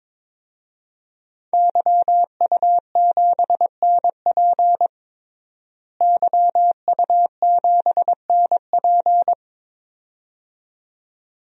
Morse code, heard as a keyed tone, about 22 wpm, 700 Hz. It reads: YU7NP YU7NP
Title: DXCC-100 Call Signs - 5 Characters - Repeated 2x - 22wpm